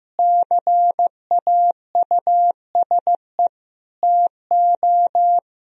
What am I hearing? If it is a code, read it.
Morse code, heard as a keyed tone, about 15 wpm, 700 Hz. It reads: CAUSE TO
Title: Sentences from Top 1000 Words - Review Part 1 - 15wpm